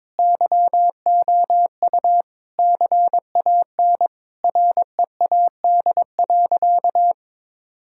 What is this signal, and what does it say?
Morse code, heard as a keyed tone, about 22 wpm, 700 Hz. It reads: YOU CAN READ.
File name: practice / 22wpm / Sentences with 8-Word Phrases - 22wpm